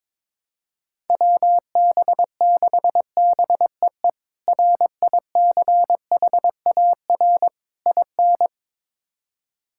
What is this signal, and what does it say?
Morse code, heard as a keyed tone, about 22 wpm, 700 Hz. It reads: WB6BEE RICHAR IN
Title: K1USN SST QSO Practice - 22wpm